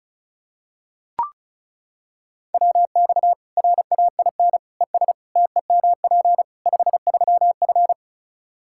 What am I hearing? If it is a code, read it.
Morse code, heard as a keyed tone, about 36 wpm, 700 Hz. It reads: WX RAIN ES TEMP 53F
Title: Practice QSO - Exchange 2 - Info - 35wpm